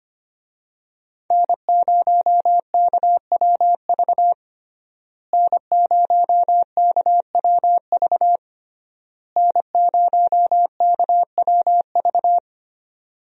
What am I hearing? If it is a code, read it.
Morse code, heard as a keyed tone, about 25 wpm, 700 Hz. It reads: N0KWV N0KWV N0KWV
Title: USA Call Signs - 5 Characters - Repeated 3x - 25wpm